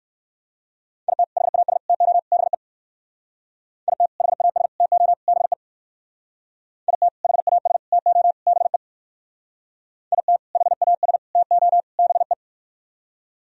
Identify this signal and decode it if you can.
Morse code, heard as a keyed tone, about 57 wpm, 700 Hz. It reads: IT HAS TO BE IT HAS TO BE IT HAS TO BE IT HAS TO BE